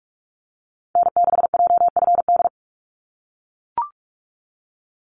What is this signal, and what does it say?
Morse code, heard as a keyed tone, about 44 wpm, 700 Hz. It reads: N6JFD
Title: NA SOTA Activator Callsigns 45wpm